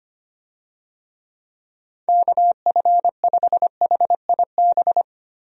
Morse code, heard as a keyed tone, about 25 wpm, 700 Hz. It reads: KF5HIB
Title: USA Call Signs - 6 Characters - 25wpm